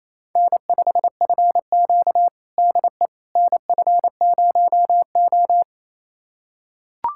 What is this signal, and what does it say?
Morse code, heard as a keyed tone, about 28 wpm, 700 Hz. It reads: N5FQ DE NF0O
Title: USA Callsign DE Callsign - 4 Characters - 28wpm